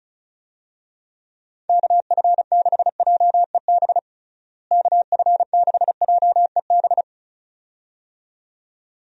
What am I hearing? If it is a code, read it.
Morse code, heard as a keyed tone, about 36 wpm, 700 Hz. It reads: KF6JEB KF6JEB
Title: USA Call Signs - 6 Characters - Repeated 2x - 35wpm